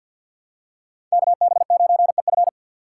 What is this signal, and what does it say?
Morse code, heard as a keyed tone, about 52 wpm, 700 Hz. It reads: KB9EF
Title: USA Call Signs - 3x Character Spacing - 50wpm